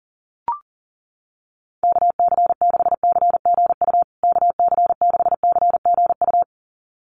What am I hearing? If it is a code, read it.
Morse code, heard as a keyed tone, about 38 wpm, 700 Hz. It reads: KC6CGU KC6CGU
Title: USA Call Signs Repeated 2x 40wpm